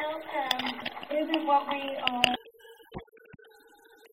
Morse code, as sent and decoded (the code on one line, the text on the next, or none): none